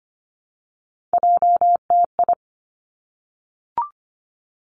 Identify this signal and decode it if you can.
Morse code, heard as a keyed tone, about 25 wpm, 700 Hz. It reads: JTI
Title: Three Challenging Letter-Number Combinations 25wpm